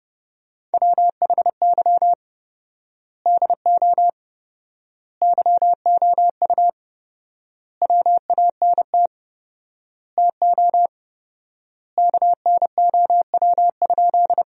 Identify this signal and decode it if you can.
Morse code, heard as a keyed tone, about 30 wpm, 700 Hz. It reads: WHY DO YOU WANT TO KNOW?
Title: 4x Word-Spacing - Sentences from Top 200 Words - 30wpm